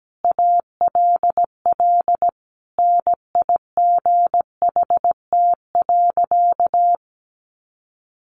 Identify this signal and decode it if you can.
Morse code, heard as a keyed tone, about 17 wpm, 700 Hz. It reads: ALL NIGHT.
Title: Sentences from Top 500 Words 17wpm